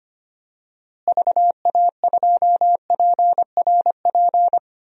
Morse code, heard as a keyed tone, about 25 wpm, 700 Hz. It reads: VA2PRP
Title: DXCC-100 Call Signs - 6 Characters - Repeated 2x - 25wpm